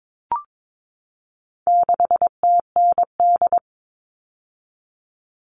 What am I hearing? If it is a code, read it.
Morse code, heard as a keyed tone, about 22 wpm, 700 Hz. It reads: 6TND
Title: All Cut Numbers - 4-Digit 22wpm